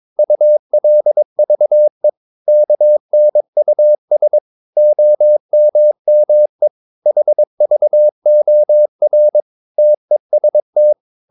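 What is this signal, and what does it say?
Morse code, heard as a keyed tone, about 22 wpm, 600 Hz. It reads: ULVE KNUS OMME HVOR TEST
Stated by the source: Morse practice of short Danish words at 22 wpm, from CW med Gnister.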